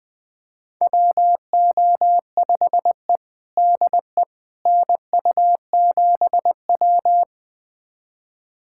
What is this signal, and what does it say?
Morse code, heard as a keyed tone, about 20 wpm, 700 Hz. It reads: WO5E DE NU7W
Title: USA Callsign DE Callsign - 4 Characters - 20wpm